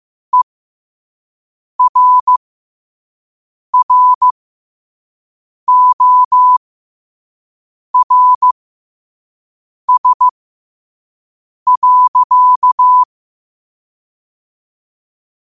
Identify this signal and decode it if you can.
Morse code, heard as a keyed tone, about 15 wpm, spaced as for 5 wpm, 1000 Hz. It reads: ERRORS.
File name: qotdfiles / 5wpm